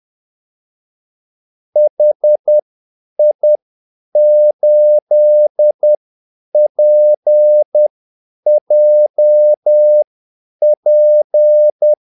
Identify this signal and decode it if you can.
Morse code, heard as a keyed tone, about 10 wpm, spaced as for 8 wpm, 600 Hz. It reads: HI8PJP
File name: Callsigns-10-8wpm